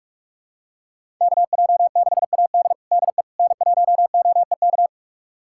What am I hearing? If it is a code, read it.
Morse code, heard as a keyed tone, about 46 wpm, 700 Hz. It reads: KJ6AD DE N1OEK
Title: USA Callsign DE Callsign - 5 Characters - 45wpm